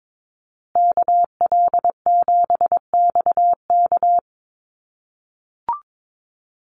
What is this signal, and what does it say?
Morse code, heard as a keyed tone, about 22 wpm, 700 Hz. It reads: KL7XK
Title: USA Call Signs Repeated 2x 22wpm